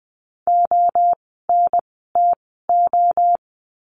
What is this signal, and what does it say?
Morse code, heard as a keyed tone, about 20 wpm, spaced as for 15 wpm, 700 Hz. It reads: ONTO
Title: 068 Lesson 12f - Letter d - Words 20wpm